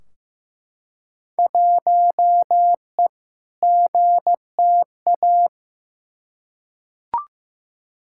Morse code, heard as a keyed tone, about 15 wpm, 700 Hz. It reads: 1E GTA
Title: Field Day Exchanges - 15wpm